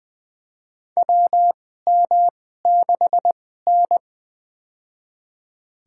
Morse code, heard as a keyed tone, about 20 wpm, spaced as for 15 wpm, 700 Hz. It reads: WM6N
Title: USA Call Signs - 4 Characters - 2x Character Spacing - 20wpm